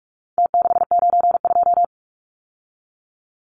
Morse code, heard as a keyed tone, about 44 wpm, 700 Hz. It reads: T692
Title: Numbers - Contest Style - 4-Digit 45wpm